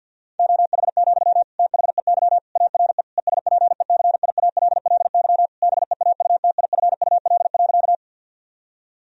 Morse code, heard as a keyed tone, about 51 wpm, 700 Hz. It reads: OH, THEY ARE ESPECIALLY BEAUTIFUL.
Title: Sentences from Top 1000 Words - Review Part 4 - 50wpm